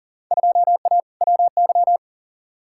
Morse code, heard as a keyed tone, about 40 wpm, 700 Hz. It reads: 2A WY
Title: Field Day Exchanges - 40wpm